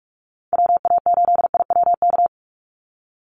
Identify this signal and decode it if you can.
Morse code, heard as a keyed tone, about 44 wpm, 700 Hz. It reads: WA8IWK
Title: Call Signs - 45wpm